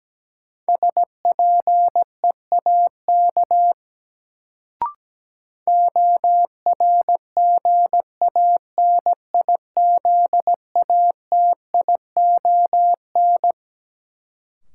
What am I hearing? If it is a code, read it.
Morse code, heard as a keyed tone, about 17 wpm, 700 Hz. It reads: SPEAK ORGANIZATION
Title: Sentences from Top 1100 Words - New Word Drill Alt - 17wpm